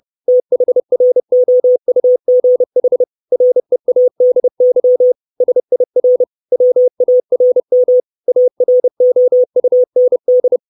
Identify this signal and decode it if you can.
Morse code, heard as a keyed tone, about 30 wpm, 500 Hz. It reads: THROUGH READY SIR WARM AROUND
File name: 632 Words 30WPM